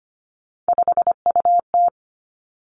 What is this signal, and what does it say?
Morse code, heard as a keyed tone, about 25 wpm, 700 Hz. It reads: 5UT
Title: Three Challenging Letter-Number Combinations 25wpm